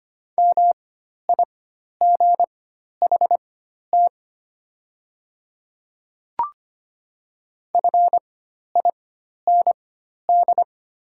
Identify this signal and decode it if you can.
Morse code, heard as a keyed tone, about 25 wpm, spaced as for 12 wpm, 690 Hz. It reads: MIGHT FIND
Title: 4x Character Spacing - Top 300 Words - 25wpm